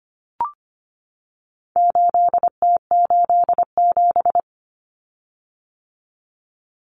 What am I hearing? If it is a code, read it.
Morse code, heard as a keyed tone, about 25 wpm, 700 Hz. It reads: 8T87
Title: Common Cut Numbers - 4-Digit 25wpm